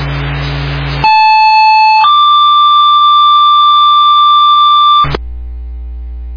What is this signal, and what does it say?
Two-tone page, heard as a one-second tone, then a three-second tone, about 870 then 1180 Hz, with noise. It is heard from 1.0 to 5.0 s.